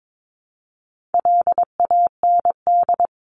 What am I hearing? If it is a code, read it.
Morse code, heard as a keyed tone, about 22 wpm, 700 Hz. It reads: LAND
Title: Top 500 Words 22wpm